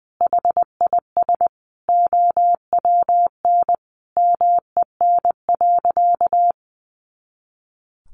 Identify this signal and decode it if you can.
Morse code, heard as a keyed tone, about 20 wpm, 700 Hz. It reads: HIS OWN MEN.
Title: Sentences from Top 300 Words 20wpm